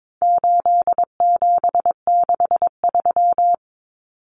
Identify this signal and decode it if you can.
Morse code, heard as a keyed tone, about 22 wpm, 700 Hz. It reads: 8763
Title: Common Cut Numbers - 4-Digit 22wpm